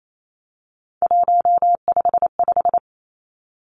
Morse code, heard as a keyed tone, about 28 wpm, 700 Hz. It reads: 155
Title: Three Challenging Letter-Number Combinations 28wpm